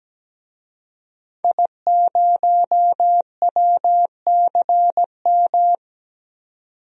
Morse code, heard as a keyed tone, about 17 wpm, 700 Hz. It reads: I0WCM
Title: DXCC-100 Call Signs - 5 Characters - 17wpm